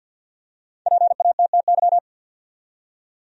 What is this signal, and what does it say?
Morse code, heard as a keyed tone, about 47 wpm, 700 Hz. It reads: PATTY
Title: USA Common Names - 50wpm